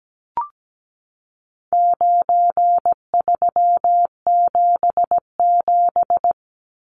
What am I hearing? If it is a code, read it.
Morse code, heard as a keyed tone, about 17 wpm, 700 Hz. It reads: 9377
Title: Numbers - 4-Digit 17wpm